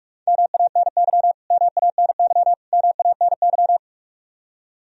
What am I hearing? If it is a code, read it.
Morse code, heard as a keyed tone, about 46 wpm, 700 Hz. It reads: MANY MANY MANY
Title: Top 100 Words Repeated 45wpm